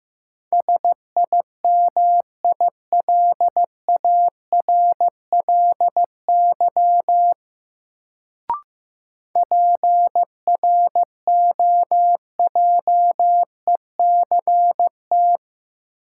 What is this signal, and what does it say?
Morse code, heard as a keyed tone, about 15 wpm, 700 Hz. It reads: SIMILARLY PROJECT
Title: Sentences from Top 1400 Words - New Word Drill - 15wpm